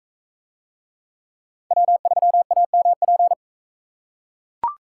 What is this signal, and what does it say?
Morse code, heard as a keyed tone, about 42 wpm, 700 Hz. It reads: W3AMP